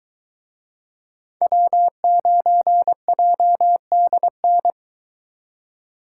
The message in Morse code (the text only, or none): W9JDN